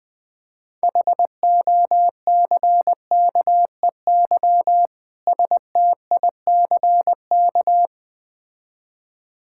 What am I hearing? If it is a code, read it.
Morse code, heard as a keyed tone, about 20 wpm, 690 Hz. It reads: HOCKEY STICK
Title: Difficult Words with Hints - 20wpm